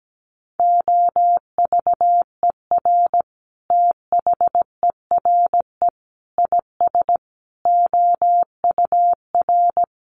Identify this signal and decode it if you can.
Morse code, heard as a keyed tone, about 17 wpm, 700 Hz. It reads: OVER THERE IS OUR
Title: Sentences from Top 1200 Words 17wpm